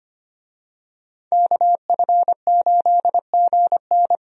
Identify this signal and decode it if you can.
Morse code, heard as a keyed tone, about 25 wpm, 700 Hz. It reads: KF8GN
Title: USA Call Signs - 5 Characters - Repeated 2x - 25wpm